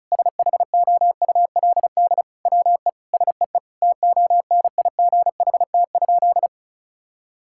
Morse code, heard as a keyed tone, about 34 wpm, 700 Hz. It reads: SHOULD WE SEE TONIGHT?
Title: Sentences from Top 1000 Words - Review Part 5 - 35wpm